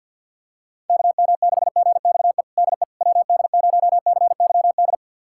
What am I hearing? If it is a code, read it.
Morse code, heard as a keyed tone, about 51 wpm, 700 Hz. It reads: KM6GXE DE WD0CYD
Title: USA Callsign DE Callsign - 6 Characters - 50wpm